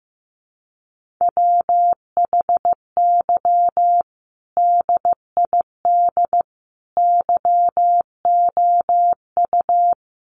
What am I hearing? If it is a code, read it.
Morse code, heard as a keyed tone, about 14 wpm, 700 Hz. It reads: WHY DID YOU
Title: Sentences from Top 1000 Words 15wpm